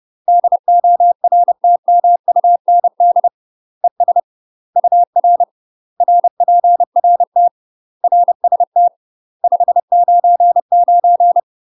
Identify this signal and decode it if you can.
Morse code, heard as a keyed tone, about 30 wpm, 700 Hz. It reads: DORTMUND ES UR RPRT RST 599
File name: UK MASTER QSOs TRAINING / 30WPM / UK MASTER QSOs 0019